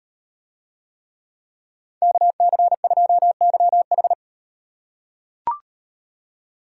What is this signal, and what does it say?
Morse code, heard as a keyed tone, about 37 wpm, 700 Hz. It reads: KC2YH